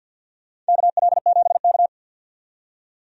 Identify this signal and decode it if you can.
Morse code, heard as a keyed tone, about 52 wpm, 700 Hz. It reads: KL7K
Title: DXCC-100 Call Signs - 4 Characters - 50wpm